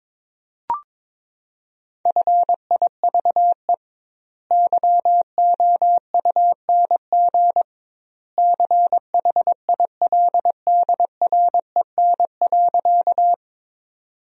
Morse code, heard as a keyed tone, about 22 wpm, 700 Hz. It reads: FIVE YOUNG CHILDREN.
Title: 2x Word-Spacing - Sentences from Top 400 Words - 22wpm